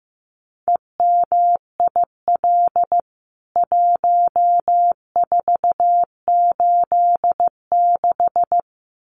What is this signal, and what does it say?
Morse code, heard as a keyed tone, about 15 wpm, 700 Hz. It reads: EMIL 1486
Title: CWT Exchange 15wpm